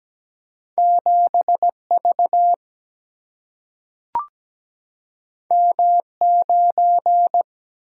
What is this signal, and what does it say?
Morse code, heard as a keyed tone, about 17 wpm, 700 Hz. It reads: 7V M9
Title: Single-Letter Single-Number 17wpm